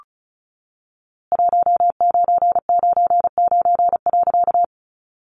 Morse code, heard as a keyed tone, about 34 wpm, 700 Hz. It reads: 1999.
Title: A Sound Of Thunder 35wpm